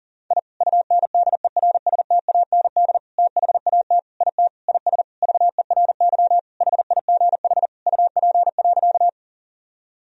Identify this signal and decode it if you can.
Morse code, heard as a keyed tone, about 40 wpm, 700 Hz. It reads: I UNDERSTAND THAT IT IS VERY HIGH UP.
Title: Sentences from Top 1000 Words - Review Part 1 - 40wpm